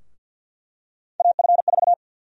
Morse code, heard as a keyed tone, about 53 wpm, 700 Hz. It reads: AF4
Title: USA Call Sign Prefixes - 50wpm